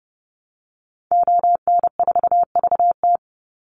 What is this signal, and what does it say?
Morse code, heard as a keyed tone, about 30 wpm, 700 Hz. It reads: ON4VT
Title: Call Signs - 30wpm